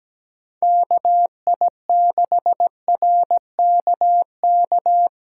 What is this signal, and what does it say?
Morse code, heard as a keyed tone, about 20 wpm, 700 Hz. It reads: KI6RKK
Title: Speed Racing - USA 2x3 Call Signs 17wpm